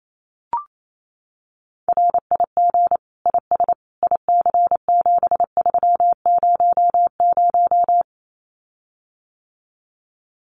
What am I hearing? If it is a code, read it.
Morse code, heard as a keyed tone, about 28 wpm, 700 Hz. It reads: RIG IS IC7300
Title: Common QSO Phrases Encore 28wpm